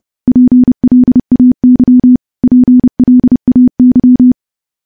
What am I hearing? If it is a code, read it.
Morse code, heard as a keyed tone, about 32 wpm, 250 Hz. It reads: PLAY PLAY
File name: morse / morse-training-30WPM